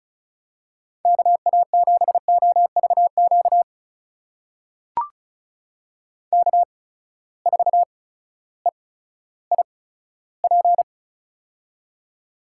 Morse code, heard as a keyed tone, about 36 wpm, spaced as for 16 wpm, 700 Hz. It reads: KA7OVQ K4EIP
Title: USA Call Signs - 8x Character Spacing - 35wpm